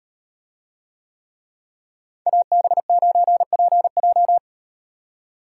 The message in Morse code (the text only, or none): AB9PJ